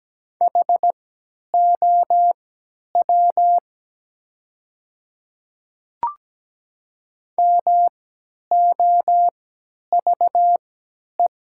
Morse code, heard as a keyed tone, about 17 wpm, spaced as for 10 wpm, 700 Hz. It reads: HOW MOVE
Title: 3x Character Spacing - Top 200 Words - 17wpm